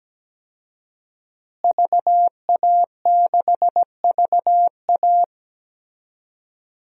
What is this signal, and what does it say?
Morse code, heard as a keyed tone, about 17 wpm, 700 Hz. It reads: VA6VA
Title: DXCC-100 Call Signs - 5 Characters - Repeated 3x - 17wpm